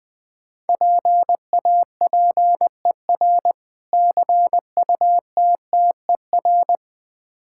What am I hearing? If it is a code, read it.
Morse code, heard as a keyed tone, about 21 wpm, 700 Hz. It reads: PAPER CUTTER
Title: Difficult Words with Hints - 20wpm